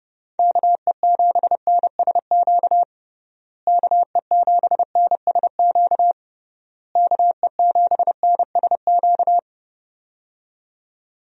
Morse code, heard as a keyed tone, about 30 wpm, 700 Hz. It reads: KE7NSQ KE7NSQ KE7NSQ
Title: USA Call Signs - 6 Characters - Repeated 3x - 30wpm